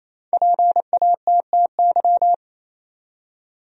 Morse code, heard as a keyed tone, about 28 wpm, 700 Hz. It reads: PATTY
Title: USA Common Names - Max 5 Letters - 28wpm